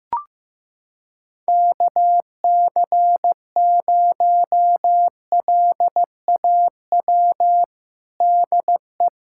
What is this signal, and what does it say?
Morse code, heard as a keyed tone, about 16 wpm, 700 Hz. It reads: KC0LAW DE
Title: USA Callsign DE Callsign - 6 Characters - 15wpm